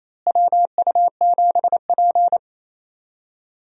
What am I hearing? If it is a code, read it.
Morse code, heard as a keyed tone, about 28 wpm, 700 Hz. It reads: WU7P
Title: USA Call Signs - 4 Characters - Repeated 2x - 28wpm